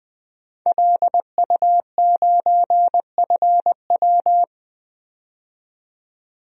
Morse code, heard as a keyed tone, about 20 wpm, 700 Hz. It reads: LU9FW
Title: DXCC-100 Call Signs - 5 Characters - 20wpm